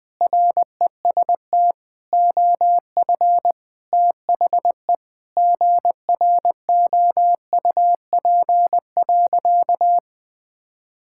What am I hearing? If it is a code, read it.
Morse code, heard as a keyed tone, about 20 wpm, 700 Hz. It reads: REST OF THE GROUP.
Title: Sentences with 5-Word Phrases - 20wpm